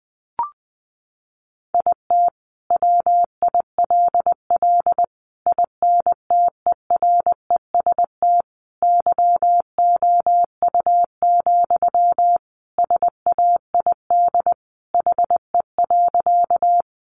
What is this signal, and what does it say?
Morse code, heard as a keyed tone, about 20 wpm, 700 Hz. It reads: IT WILL INTEREST YOU, SAID HE.
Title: Sentences from Top 400 Words 20wpm